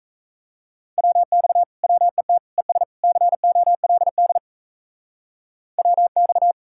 Morse code, heard as a keyed tone, about 34 wpm, 700 Hz. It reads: WX WET ES COLD WX
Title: Speed-Racing Common QSO Phrases - 28wpm